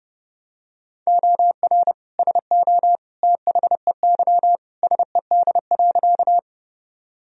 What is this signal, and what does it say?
Morse code, heard as a keyed tone, about 32 wpm, 700 Hz. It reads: OR SO THEY SED.
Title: Sentences from Top 100 Words and Abbreviations 30wpm